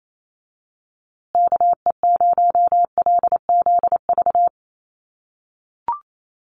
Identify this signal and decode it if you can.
Morse code, heard as a keyed tone, about 28 wpm, 700 Hz. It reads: KE0LZV
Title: USA Call Signs Repeated 2x 28wpm